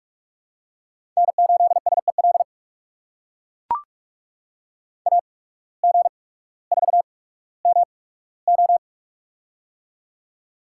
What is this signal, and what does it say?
Morse code, heard as a keyed tone, about 44 wpm, spaced as for 19 wpm, 700 Hz. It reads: N8SEL AG4MO
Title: USA Call Signs - 8x Character Spacing - 45wpm